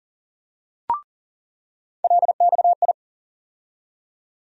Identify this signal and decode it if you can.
Morse code, heard as a keyed tone, about 40 wpm, 700 Hz. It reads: LXI
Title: USA Call Sign Postfixes - 3 Letters 40wpm